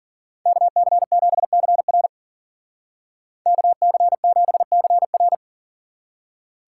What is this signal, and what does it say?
Morse code, heard as a keyed tone, about 49 wpm, 700 Hz. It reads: KC7CR KC7CR
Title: Speed Racing - USA Call Signs 40wpm